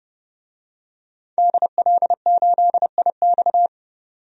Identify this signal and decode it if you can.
Morse code, heard as a keyed tone, about 32 wpm, 700 Hz. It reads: DL8IX
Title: DXCC-100 Call Signs - 5 Characters - 30wpm